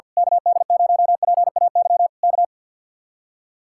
Morse code, heard as a keyed tone, about 49 wpm, 700 Hz. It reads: KD0PAY K
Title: Speed-Racing Common QSO Phrases - 50wpm